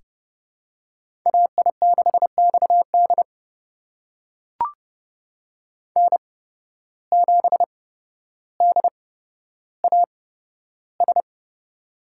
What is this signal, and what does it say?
Morse code, heard as a keyed tone, about 29 wpm, spaced as for 13 wpm, 700 Hz. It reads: AI6XD N7DAS
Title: USA Call Signs - 5 Characters - 8x Character Spacing - 30wpm